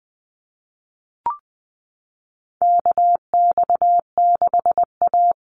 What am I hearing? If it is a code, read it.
Morse code, heard as a keyed tone, about 20 wpm, 700 Hz. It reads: KX6A
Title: NA SOTA Chaser Callsigns 20wpm